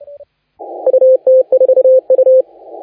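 Morse code, heard as a keyed tone, about 31 wpm, 520 Hz. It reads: UT4U
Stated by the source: Morse contest audio recorded by LA4C during the SAC CW 2018 contest.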